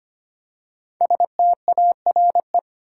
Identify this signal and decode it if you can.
Morse code, heard as a keyed tone, about 25 wpm, 700 Hz. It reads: STARE
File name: 5 Letter Words - First 3 Letters Given - 25wpm